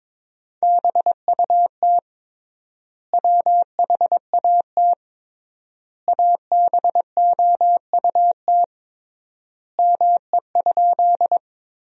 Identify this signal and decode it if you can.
Morse code, heard as a keyed tone, about 22 wpm, 700 Hz. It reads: BUT WHAT ABOUT ME?
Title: 3x Word-Spacing - Sentences from Top 200 Words - 22wpm